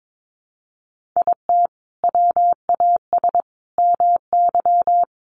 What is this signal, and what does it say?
Morse code, heard as a keyed tone, about 22 wpm, 700 Hz. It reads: IT WAS MY
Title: Sentences from Top 1500 Words 22wpm